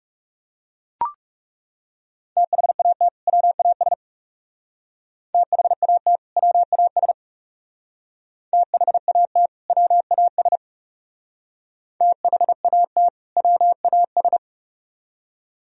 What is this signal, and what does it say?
Morse code, heard as a keyed tone, about 39 wpm, 700 Hz. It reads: THAT WAS THAT WAS THAT WAS THAT WAS